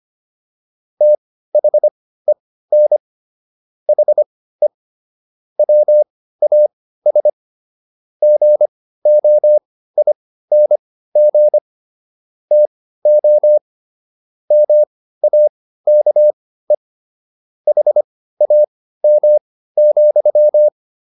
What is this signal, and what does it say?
Morse code, heard as a keyed tone, about 25 wpm, spaced as for 15 wpm, 600 Hz. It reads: THEN HE WAS GOING TO MAKE HAM,